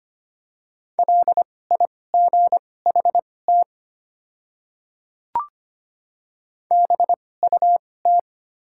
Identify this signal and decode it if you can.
Morse code, heard as a keyed tone, about 25 wpm, spaced as for 18 wpm, 700 Hz. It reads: LIGHT BUT
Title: 2x Character Spacing - Top 300 Words - 25wpm